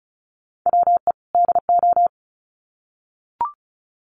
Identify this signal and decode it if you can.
Morse code, heard as a keyed tone, about 34 wpm, 700 Hz. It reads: WE DO
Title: Sets of 2 Words 35wpm